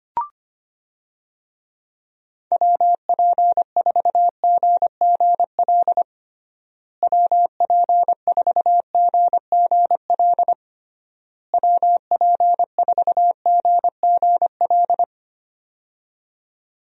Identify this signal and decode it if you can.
Morse code, heard as a keyed tone, about 25 wpm, 700 Hz. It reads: WP4GGL WP4GGL WP4GGL
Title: USA Call Signs - 6 Characters - Repeated 3x - 25wpm